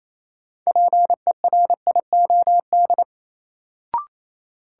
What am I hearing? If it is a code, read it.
Morse code, heard as a keyed tone, about 28 wpm, 700 Hz. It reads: PERIOD
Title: Sets of 5 Words - New Word Drill - 28wpm